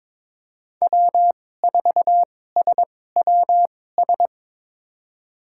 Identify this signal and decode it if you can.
Morse code, heard as a keyed tone, about 22 wpm, spaced as for 16 wpm, 700 Hz. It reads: W4SWS
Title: USA Call Signs - 5 Characters - 2x Character Spacing - 22wpm